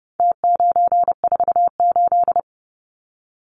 Morse code, heard as a keyed tone, about 30 wpm, 700 Hz. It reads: T948
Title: Numbers - Contest Style - 4-Digit 30wpm